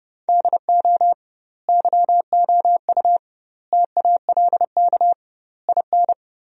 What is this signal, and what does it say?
Morse code, heard as a keyed tone, about 32 wpm, 700 Hz. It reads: DO YOU TALK IN
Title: 2x Word-Spacing - Sentences from Top 500 Words - 30wpm